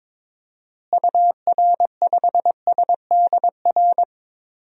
Morse code, heard as a keyed tone, about 22 wpm, 700 Hz. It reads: UR5SDR
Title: DXCC-100 Call Signs - 6 Characters - Repeated 3x - 22wpm